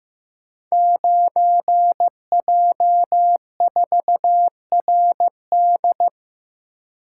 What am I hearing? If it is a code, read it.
Morse code, heard as a keyed tone, about 15 wpm, 700 Hz. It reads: 9J4RD
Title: Code Groups - 5 Letter:Numbers - 15wpm